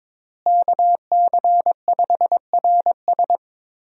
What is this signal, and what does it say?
Morse code, heard as a keyed tone, about 21 wpm, 700 Hz. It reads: KC5RS
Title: USA Call Signs - 5 Characters - Repeated 2x - 22wpm